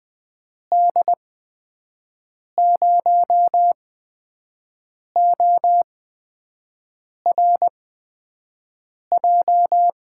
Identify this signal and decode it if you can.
Morse code, heard as a keyed tone, about 20 wpm, spaced as for 5 wpm, 700 Hz. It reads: D0ORJ
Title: USA Call Signs - 6 Characters - 8x Character Spacing - 20wpm